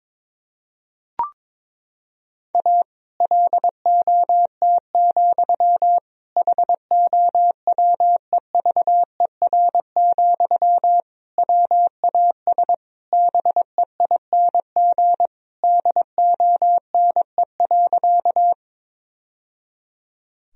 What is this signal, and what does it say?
Morse code, heard as a keyed tone, about 22 wpm, 700 Hz. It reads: A LOT, HOWEVER, WAS BEING DONE.
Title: Sentences from Top 1000 Words - Review Part 3 - 22wpm